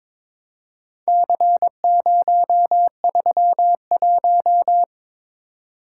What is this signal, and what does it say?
Morse code, heard as a keyed tone, about 22 wpm, 690 Hz. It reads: C031
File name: practice / 22wpm / Four Letter-Number Combinations - 22wpm